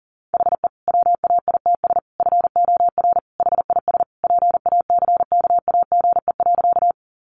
Morse code, heard as a keyed tone, about 40 wpm, 700 Hz. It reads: HE WAITS FOR HIS PACKAGE.
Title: Additional Easy Phrases 40wpm